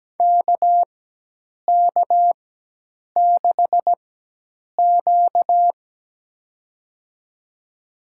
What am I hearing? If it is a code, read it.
Morse code, heard as a keyed tone, about 17 wpm, spaced as for 8 wpm, 700 Hz. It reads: KK6Q
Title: USA Call Signs - 4x Character Spacing - 17wpm